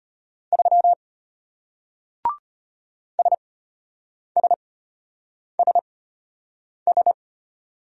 Morse code, heard as a keyed tone, about 34 wpm, spaced as for 8 wpm, 700 Hz. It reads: 3 SSSS